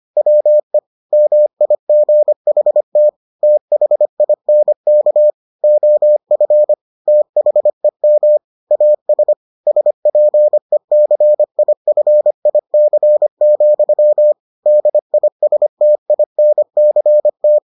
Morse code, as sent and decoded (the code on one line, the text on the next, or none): .-- . / -- .. --. .... - / - .... .. -. -.- / --- ..-. / - .... -- / .- ... / ... .--. -.-. .. ..-. .. -.-. --..-- / -.. .. ... - .. -. -.-. -
WE MIGHT THINK OF THEM AS SPECIFIC, DISTINCT